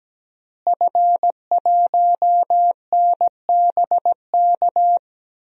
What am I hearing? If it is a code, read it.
Morse code, heard as a keyed tone, about 17 wpm, 700 Hz. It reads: F1NBK
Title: DXCC-100 Call Signs - 5 Characters - Repeated 2x - 17wpm